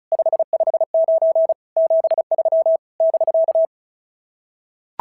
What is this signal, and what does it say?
Morse code, heard as a keyed tone, about 36 wpm, 690 Hz. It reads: 559 73 <BK>
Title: SOTA QSO Practice 35wpm